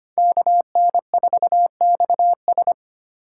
Morse code, heard as a keyed tone, about 25 wpm, 700 Hz. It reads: KN4XS
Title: USA Call Signs - 5 Characters - 3x Character Spacing - 25wpm